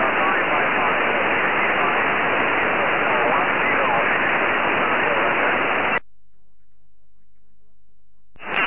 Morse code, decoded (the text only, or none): none